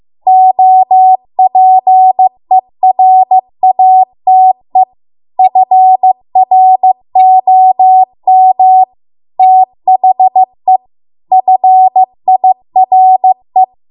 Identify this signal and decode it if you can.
Morse code, heard as a keyed tone, about 15 wpm, 750 Hz. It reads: OPERATE FROM THE FIRE